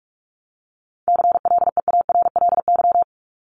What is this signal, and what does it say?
Morse code, heard as a keyed tone, about 44 wpm, 700 Hz. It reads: CLEARLY
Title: LY Suffix - Top 100 Words with Base Words 45wpm